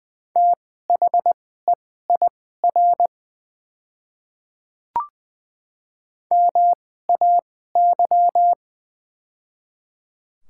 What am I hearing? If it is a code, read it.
Morse code, heard as a keyed tone, about 20 wpm, spaced as for 15 wpm, 700 Hz. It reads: THEIR MAY
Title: 2x Character Spacing - Top 100 Words - 20wpm